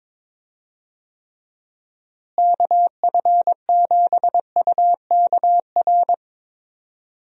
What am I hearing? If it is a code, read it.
Morse code, heard as a keyed tone, about 21 wpm, 700 Hz. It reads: KF7UKR
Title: USA Call Signs - 6 Characters - Repeated 2x - 22wpm